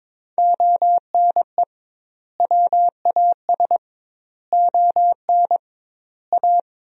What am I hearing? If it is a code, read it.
Morse code, heard as a keyed tone, about 21 wpm, 700 Hz. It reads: ONE WAS ON A
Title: 2x Word-Spacing - Sentences from Top 300 Words - 22wpm